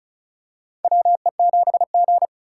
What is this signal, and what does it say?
Morse code, heard as a keyed tone, about 34 wpm, 700 Hz. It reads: WE7G
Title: K1USN SST QSO Practice - 35wpm